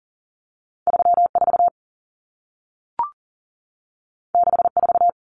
Morse code, heard as a keyed tone, about 40 wpm, 700 Hz. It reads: 34 64
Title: Numbers - 2-Digit 40wpm